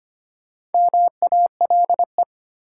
Morse code, heard as a keyed tone, about 25 wpm, 700 Hz. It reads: MALE
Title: Four Letter Words - 25wpm